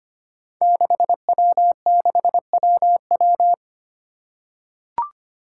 Morse code, heard as a keyed tone, about 25 wpm, 700 Hz. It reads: 6W6WW
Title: Five Challenging Letter-Number Combinations 25wpm